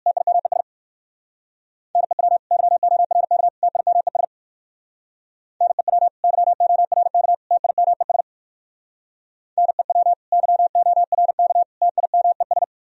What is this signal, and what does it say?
Morse code, heard as a keyed tone, about 60 wpm, 700 Hz. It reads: TIMES NEW YORK TIMES NEW YORK TIMES NEW YORK TIMES